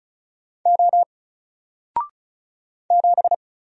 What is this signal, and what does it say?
Morse code, heard as a keyed tone, about 34 wpm, 700 Hz. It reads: O 7
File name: Single-Letter-Number-35wpm